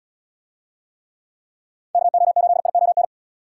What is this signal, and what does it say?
Morse code, heard as a keyed tone, about 72 wpm, 700 Hz. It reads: KO8ION